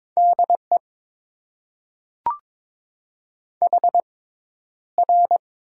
Morse code, heard as a keyed tone, about 22 wpm, spaced as for 11 wpm, 700 Hz. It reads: DE HR
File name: Top 90 QSO Elements - 6x Character Spacing - 22wpm